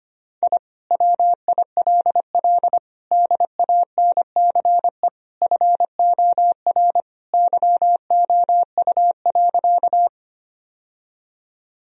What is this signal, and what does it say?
Morse code, heard as a keyed tone, about 25 wpm, 690 Hz. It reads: I WILL DANCE FOR YOU.
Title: Sentences from Top 1000 Words - Review Part 4 - 25wpm